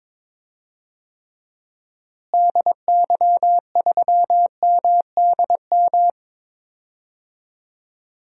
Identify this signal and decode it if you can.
Morse code, heard as a keyed tone, about 22 wpm, 700 Hz. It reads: DY3MDM
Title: DXCC-100 Call Signs - 6 Characters - Repeated 2x - 22wpm